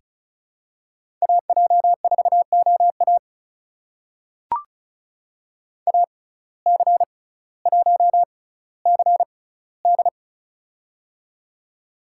Morse code, heard as a keyed tone, about 34 wpm, spaced as for 18 wpm, 700 Hz. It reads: AJ4OA AC1CD
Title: USA Call Signs - 5 Characters - 6x Character Spacing - 35wpm